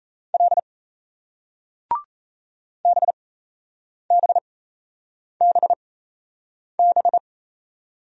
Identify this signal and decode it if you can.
Morse code, heard as a keyed tone, about 38 wpm, spaced as for 8 wpm, 700 Hz. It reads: L BBBB